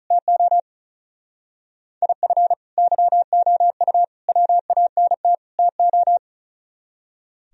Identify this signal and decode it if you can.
Morse code, heard as a keyed tone, about 44 wpm, 700 Hz. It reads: TO IF YOU WANT TO
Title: Speed-Racing - Sets of 4 Words 35wpm